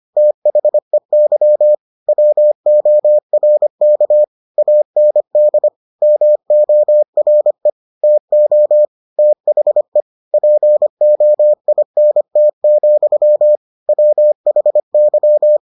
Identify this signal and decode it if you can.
Morse code, heard as a keyed tone, about 25 wpm, 600 Hz. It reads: THEY WORK AND MORE TO THE POINT, WHY